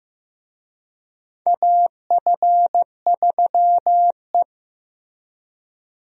Morse code, heard as a keyed tone, about 16 wpm, 700 Hz. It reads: AF3E
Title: USA Call Signs - 4 Characters - Repeated 3x - 15wpm